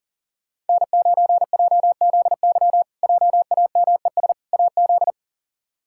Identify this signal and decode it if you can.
Morse code, heard as a keyed tone, about 40 wpm, 700 Hz. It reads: N9JZY JAMES AZ
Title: K1USN SST QSO Practice - 40wpm